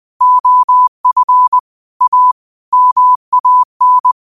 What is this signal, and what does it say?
Morse code, heard as a keyed tone, about 20 wpm, 1000 Hz. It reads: OF A MAN
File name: 20wpm